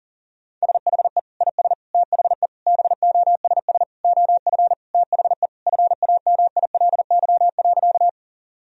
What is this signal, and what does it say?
Morse code, heard as a keyed tone, about 37 wpm, 700 Hz. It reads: SHE IS THE BOSS OF THE FAMILY.
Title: Sentences from Top 1000 Words - Review Part 4 - 40wpm